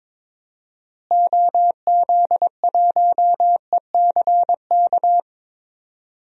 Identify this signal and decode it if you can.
Morse code, heard as a keyed tone, about 22 wpm, 700 Hz. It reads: OZ1ECK
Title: DXCC-100 Call Signs - 6 Characters - Repeated 3x - 22wpm